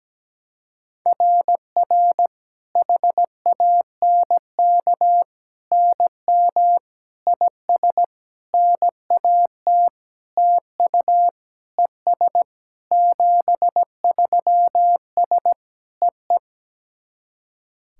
Morse code, heard as a keyed tone, about 17 wpm, 700 Hz. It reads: RR HANK NM IS NAT TU ES 73S EE
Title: SOTA QSO Practice 17wpm